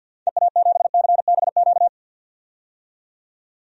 Morse code, heard as a keyed tone, about 47 wpm, 700 Hz. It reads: EA7CBQ
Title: DXCC-100 Call Signs - 6 Characters - 50wpm